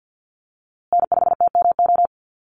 Morse code, heard as a keyed tone, about 50 wpm, 700 Hz. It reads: N5TMO
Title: USA Call Signs Repeated 3x 50wpm